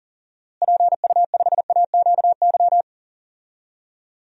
Morse code, heard as a keyed tone, about 37 wpm, 700 Hz. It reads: PU5AQY
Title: DXCC-100 Call Signs - 6 Characters - Repeated 2x - 40wpm